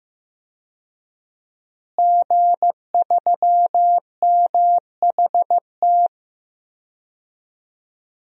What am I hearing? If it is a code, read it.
Morse code, heard as a keyed tone, about 15 wpm, 700 Hz. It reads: G3MHT
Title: DXCC-100 Call Signs - 5 Characters - Repeated 3x - 15wpm